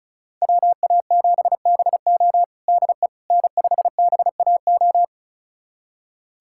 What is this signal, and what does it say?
Morse code, heard as a keyed tone, about 36 wpm, 700 Hz. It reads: WA7BO DE N5BAO
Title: USA Callsign DE Callsign - 5 Characters - 35wpm